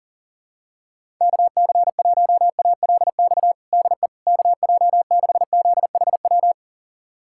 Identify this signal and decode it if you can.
Morse code, heard as a keyed tone, about 40 wpm, 700 Hz. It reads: KC1ALX DE KJ6ZHW
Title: USA Callsign DE Callsign - 6 Characters - 40wpm